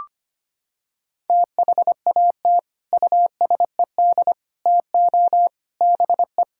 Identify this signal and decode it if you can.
Morse code, heard as a keyed tone, about 25 wpm, 700 Hz. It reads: THAT USED TO BE
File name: practice / 25wpm / Sentences with 3-Word Phrases - 25wpm